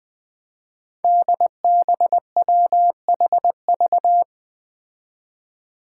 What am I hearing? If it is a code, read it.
Morse code, heard as a keyed tone, about 20 wpm, 700 Hz. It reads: DBWHV
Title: Five Challenging Letter-Number Combinations 20wpm